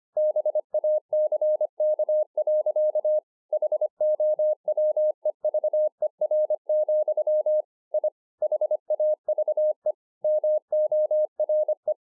Morse code, heard as a keyed tone, about 25 wpm, 610 Hz. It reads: BACK. HOWEVER, I HAVE MORE
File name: 2011-04-mp3 / THIS IS NOT MY SHACK 2 25WPM